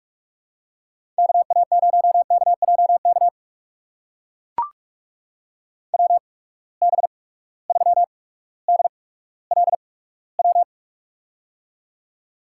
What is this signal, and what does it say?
Morse code, heard as a keyed tone, about 43 wpm, spaced as for 19 wpm, 700 Hz. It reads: KA0KJK WB3DLW
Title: USA Call Signs - 6 Characters - 8x Character Spacing - 45wpm